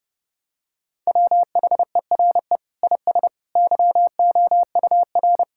Morse code, heard as a keyed tone, about 30 wpm, 700 Hz. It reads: WHERE IS YOUR
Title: Sentences from Top 1600 Words 30wpm